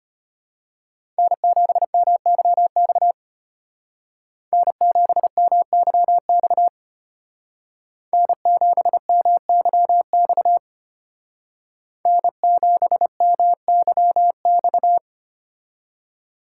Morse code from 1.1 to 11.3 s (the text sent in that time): N7MYX N7MYX N7MYX